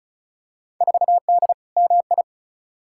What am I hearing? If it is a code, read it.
Morse code, heard as a keyed tone, about 36 wpm, 700 Hz. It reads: 4D MI
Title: Field Day Exchanges Encore - 35wpm